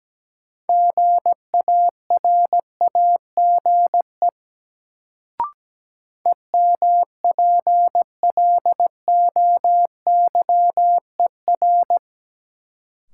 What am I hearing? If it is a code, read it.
Morse code, heard as a keyed tone, about 17 wpm, 700 Hz. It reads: GARAGE EMPLOYER
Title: Sentences from Top 1600 Words - New Word Drill - 17wpm